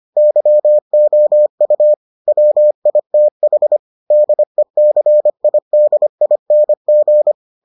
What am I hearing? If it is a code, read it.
Morse code, heard as a keyed tone, about 25 wpm, 600 Hz. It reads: YOU WITH DECIDING